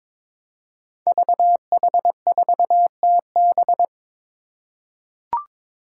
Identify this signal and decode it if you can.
Morse code, heard as a keyed tone, about 21 wpm, 700 Hz. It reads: VH4TB
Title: Five Challenging Letter-Number Combinations 22wpm